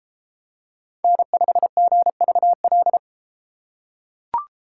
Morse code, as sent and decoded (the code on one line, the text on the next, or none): -. ..... --. ...- .-..
N5GVL